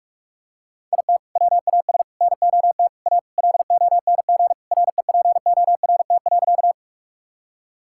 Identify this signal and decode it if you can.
Morse code, heard as a keyed tone, about 45 wpm, 700 Hz. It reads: IT WAS NOT A LONG REPORT.
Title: Sentences from Top 1000 Words - Review Part 6 - 45wpm